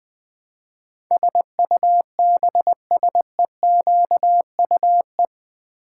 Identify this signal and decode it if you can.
Morse code, heard as a keyed tone, about 20 wpm, 700 Hz. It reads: SUBSEQUE
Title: Build A Word - 4 Letters At A Time - 10-Letter Words - 20wpm